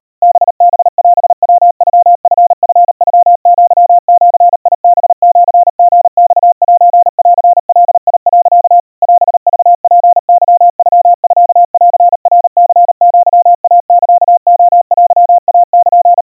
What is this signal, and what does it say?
Morse code (Japanese Wabun code), heard as a keyed tone, about 40 wpm, 700 Hz. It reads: ハハオヤノチチノアジハシリマセンガ、 オクツケノミンナニアイサレテイル
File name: kawa / Morse sample / kamyu 140 cpm 700 Hz Crispy